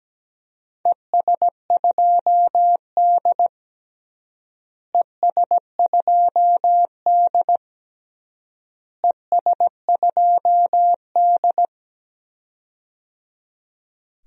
Morse code, heard as a keyed tone, about 17 wpm, 700 Hz. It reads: ES2D ES2D ES2D
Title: DXCC-100 Call Signs - 4 Characters - Repeated 3x - 17wpm